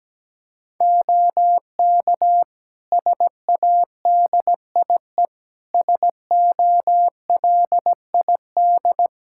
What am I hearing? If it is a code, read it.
Morse code, heard as a keyed tone, about 17 wpm, 700 Hz. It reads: OK SADIE SOLID
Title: Practice QSO - Exchange 3 - More Info - 17wpm